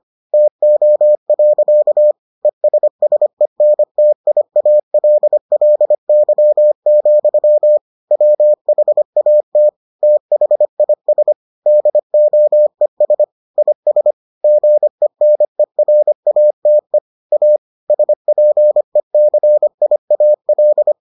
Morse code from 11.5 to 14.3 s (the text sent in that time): DOES IS